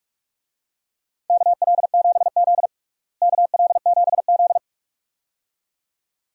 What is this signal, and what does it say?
Morse code, heard as a keyed tone, about 46 wpm, 700 Hz. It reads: KL7Z KL7Z
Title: DXCC-100 Call Signs - 4 Characters - Repeated 2x - 45wpm